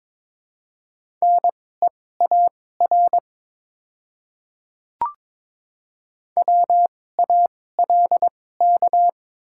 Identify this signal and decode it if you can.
Morse code, heard as a keyed tone, about 22 wpm, spaced as for 16 wpm, 700 Hz. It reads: NEAR WALK
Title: 2x Character Spacing - Top 300 Words - 22wpm